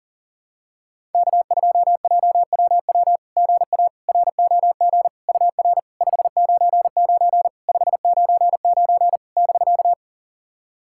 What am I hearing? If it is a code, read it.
Morse code, heard as a keyed tone, about 37 wpm, 690 Hz. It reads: K2JWW GA ROG UR 599 599 <BK>
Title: SOTA QSO Practice 40wpm